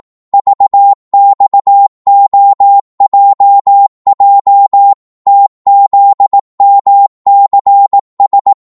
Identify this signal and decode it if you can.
Morse code, heard as a keyed tone, about 18 wpm, 800 Hz. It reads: VXOJJ TZMCS
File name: Morse / ltr-18wpm